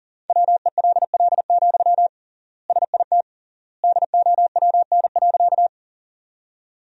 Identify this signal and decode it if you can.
Morse code, heard as a keyed tone, about 40 wpm, 700 Hz. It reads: WELL, SIT DOWN.
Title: 3x Word-Spacing - Sentences from Top 500 Words - 40wpm